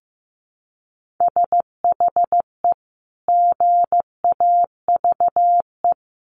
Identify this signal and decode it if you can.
Morse code, heard as a keyed tone, about 15 wpm, 700 Hz. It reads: SHE GAVE
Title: Sentences from Top 1400 Words 15wpm